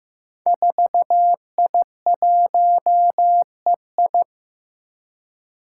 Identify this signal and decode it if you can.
Morse code, heard as a keyed tone, about 15 wpm, 700 Hz. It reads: 4I1EI
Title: Five Challenging Letter-Number Combinations 15wpm